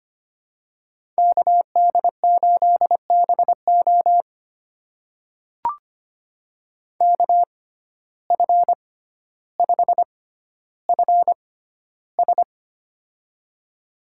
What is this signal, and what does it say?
Morse code, heard as a keyed tone, about 25 wpm, spaced as for 13 wpm, 690 Hz. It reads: KD8BO KF5FS
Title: USA Call Signs - 5 Characters - 6x Character Spacing - 25wpm